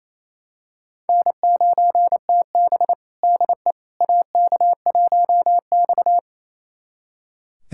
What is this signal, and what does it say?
Morse code, heard as a keyed tone, about 28 wpm, 700 Hz. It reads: N9TB DE AK1X
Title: USA Callsign DE Callsign - 4 Characters - 28wpm